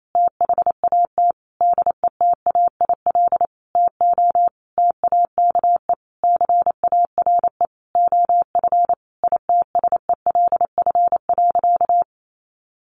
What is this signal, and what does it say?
Morse code, heard as a keyed tone, about 28 wpm, 700 Hz. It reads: THAT DETAIL TO TAKE CARE OF ITSELF.
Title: Sentences from Top 1500 Words 28wpm